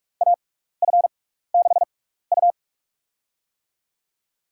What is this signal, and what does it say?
Morse code, heard as a keyed tone, about 42 wpm, spaced as for 15 wpm, 700 Hz. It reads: AF6U
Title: USA Call Signs - 6x Character Spacing - 45wpm